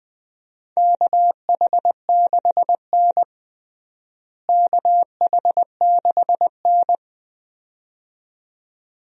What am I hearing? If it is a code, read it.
Morse code, heard as a keyed tone, about 21 wpm, 700 Hz. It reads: KH6N KH6N
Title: DXCC-100 Call Signs - 4 Characters - Repeated 2x - 20wpm